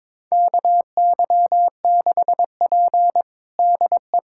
Morse code, heard as a keyed tone, about 22 wpm, 700 Hz. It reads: KY6P DE
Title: USA Callsign DE Callsign - 4 Characters - 22wpm